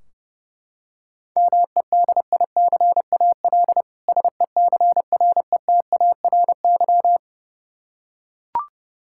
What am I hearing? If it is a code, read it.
Morse code, heard as a keyed tone, about 32 wpm, 700 Hz. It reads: MEDICAL SECRETARY
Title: Practice QSO - Exchange 3 - More Info - 30wpm